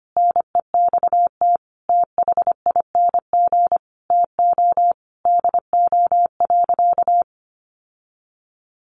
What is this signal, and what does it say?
Morse code, heard as a keyed tone, about 25 wpm, 700 Hz. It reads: NEXT THING TO DO.
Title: Sentences from Top 1300 Words 25wpm